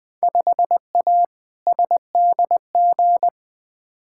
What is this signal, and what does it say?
Morse code, heard as a keyed tone, about 20 wpm, 700 Hz. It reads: 5A SDG
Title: Field Day Exchanges Encore - 20wpm